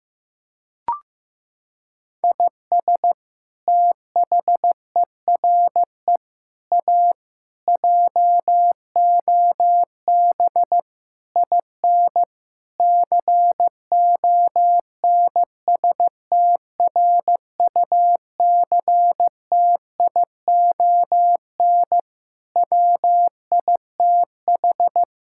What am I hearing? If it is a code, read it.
Morse code, heard as a keyed tone, about 15 wpm, 700 Hz. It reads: IS THERE A JOB IN CONSTRUCTION WITH
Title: Sentences from Top 1600 Words 15wpm